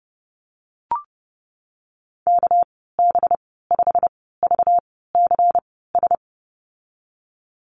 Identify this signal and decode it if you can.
Morse code, heard as a keyed tone, about 32 wpm, spaced as for 17 wpm, 700 Hz. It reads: KB5VCS
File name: Lesson 37g - Letter x - Call Signs 30wpm